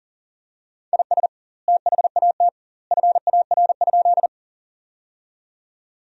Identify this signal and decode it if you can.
Morse code, heard as a keyed tone, about 40 wpm, 700 Hz. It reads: IS THAT FAR?
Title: 2x Word-Spacing - Sentences from Top 300 Words - 40wpm